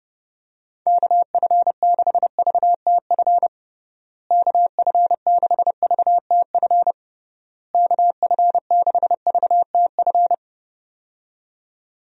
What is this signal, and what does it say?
Morse code, heard as a keyed tone, about 32 wpm, 700 Hz. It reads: KF6VTF KF6VTF KF6VTF
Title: USA Call Signs - 6 Characters - Repeated 3x - 30wpm